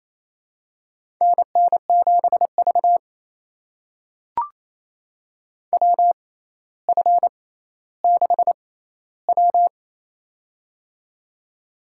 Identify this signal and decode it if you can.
Morse code, heard as a keyed tone, about 28 wpm, spaced as for 14 wpm, 700 Hz. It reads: NN7V WF6W
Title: USA Call Signs - 4 Characters - 6x Character Spacing - 28wpm